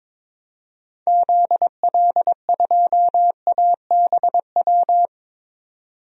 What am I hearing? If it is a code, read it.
Morse code, heard as a keyed tone, about 22 wpm, 700 Hz. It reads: ZL2ABW
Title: DXCC-100 Call Signs - 6 Characters - 22wpm